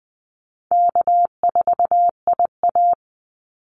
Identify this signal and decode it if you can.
Morse code, heard as a keyed tone, about 20 wpm, 700 Hz. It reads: K4IA
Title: Easy Call Signs - 20wpm